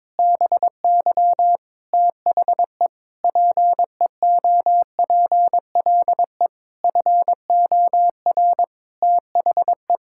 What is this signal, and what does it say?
Morse code, heard as a keyed tone, about 22 wpm, 700 Hz. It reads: BY THE PEOPLE FOR THE
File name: Sets of 9 Words - 22wpm